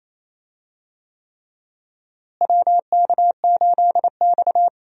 Morse code, heard as a keyed tone, about 27 wpm, 700 Hz. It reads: WK8X
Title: USA Call Signs - 4 Characters - Repeated 3x - 28wpm